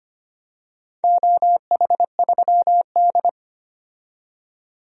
Morse code, heard as a keyed tone, about 25 wpm, 700 Hz. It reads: OH3D
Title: DXCC-100 Call Signs - 4 Characters - 25wpm